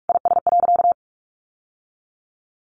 Morse code, heard as a keyed tone, about 44 wpm, 700 Hz. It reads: IS.
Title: Sentences from Top 1100 Words 45wpm